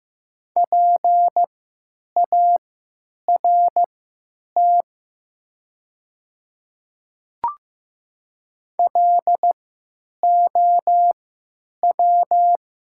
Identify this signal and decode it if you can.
Morse code, heard as a keyed tone, about 15 wpm, spaced as for 8 wpm, 700 Hz. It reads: PART LOW
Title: 3x Character Spacing - Top 300 Words - 15wpm